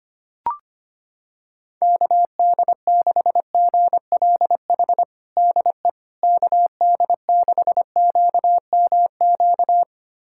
Morse code, heard as a keyed tone, about 25 wpm, 700 Hz. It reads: KD6GLH DE KD6QMQ
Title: USA Callsign DE Callsign - 6 Characters - 25wpm